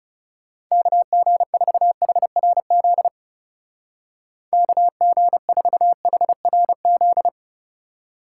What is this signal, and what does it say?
Morse code, heard as a keyed tone, about 37 wpm, 700 Hz. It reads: KG4HRZ KG4HRZ
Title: Speed Racing - USA Call Signs 30wpm